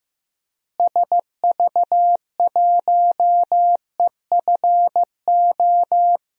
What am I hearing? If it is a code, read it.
Morse code, heard as a keyed tone, about 15 wpm, 700 Hz. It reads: SV1EFO
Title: DXCC-100 Call Signs - 6 Characters - Repeated 3x - 15wpm